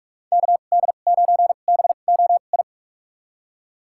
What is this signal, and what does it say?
Morse code, heard as a keyed tone, about 46 wpm, spaced as for 32 wpm, 700 Hz. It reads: KD9BOI
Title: USA Call Signs - 6 Characters - 2x Character Spacing - 45wpm